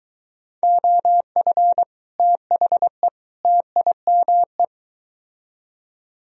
Morse code, heard as a keyed tone, about 23 wpm, 700 Hz. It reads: OF THE TIME